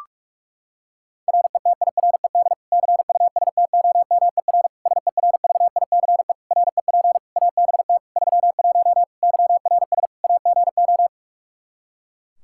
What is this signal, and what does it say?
Morse code, heard as a keyed tone, about 46 wpm, 700 Hz. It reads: RETIRED CUSTOMER SERVICE REP ABT 31 YRS AGO
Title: Practice QSO - Exchange 3 - More Info - 45wpm